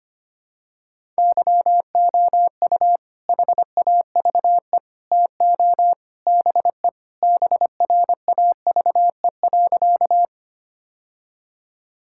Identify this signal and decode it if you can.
Morse code, heard as a keyed tone, about 25 wpm, 700 Hz. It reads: YOU HAVE TO BE BRAVE.
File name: practice / 25wpm / Sentences with 3-Word Phrases - 25wpm